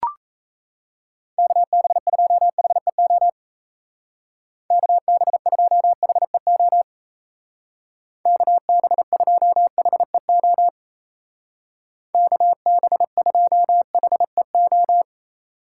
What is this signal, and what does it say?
Morse code, heard as a keyed tone, about 34 wpm, 700 Hz. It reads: KB2HEO KB2HEO KB2HEO KB2HEO
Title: Speed Racing - USA 2x3 Call Signs 28wpm